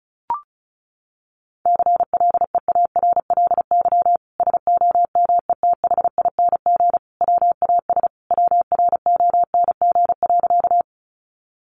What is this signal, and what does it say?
Morse code, heard as a keyed tone, about 34 wpm, 700 Hz. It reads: CLEARLY SOMETHING WAS WRONG.
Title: Sentences from Top 1300 Words 35wpm